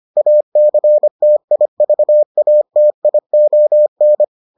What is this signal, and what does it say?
Morse code, heard as a keyed tone, about 25 wpm, 600 Hz. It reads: ACTIVATION